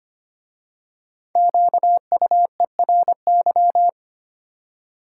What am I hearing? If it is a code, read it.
Morse code, heard as a keyed tone, about 25 wpm, 700 Hz. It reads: QUERY
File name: 5 Letter Words - First 3 Letters Given - 25wpm